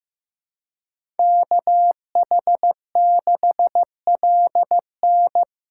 Morse code, heard as a keyed tone, about 18 wpm, 700 Hz. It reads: KH6LN
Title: Speed Racing - USA 2x2 Call Signs 15wpm